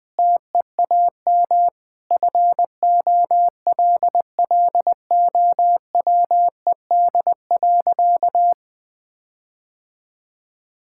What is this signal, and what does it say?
Morse code, heard as a keyed tone, about 21 wpm, 700 Hz. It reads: TEAM FOLLOWED.
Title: Sentences with 5-Word Phrases - 20wpm